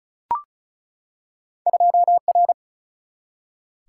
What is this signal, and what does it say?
Morse code, heard as a keyed tone, about 35 wpm, 700 Hz. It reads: 2R
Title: Single-Letter Single-Number 35wpm